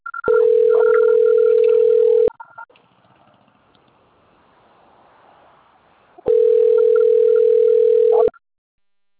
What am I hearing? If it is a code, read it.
Ringback tone (North America), 440 + 480 Hz.